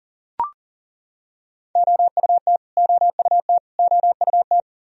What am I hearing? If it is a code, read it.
Morse code, heard as a keyed tone, about 39 wpm, 700 Hz. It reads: OUT OUT OUT
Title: Top 100 Words Repeated 40wpm